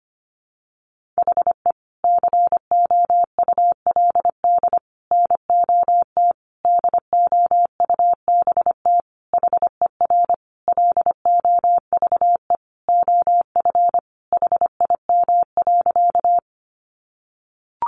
Morse code, heard as a keyed tone, about 25 wpm, 700 Hz. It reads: HE COULD NOT DOUBT HER LOVE OF HIM.
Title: Sentences from Top 1500 Words 25wpm